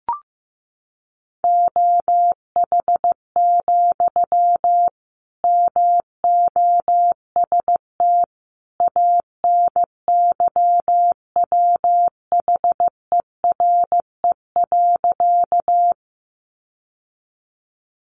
Morse code, heard as a keyed tone, about 14 wpm, 700 Hz. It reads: OH, MOST ANYWHERE.
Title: Sentences from Top 1400 Words 15wpm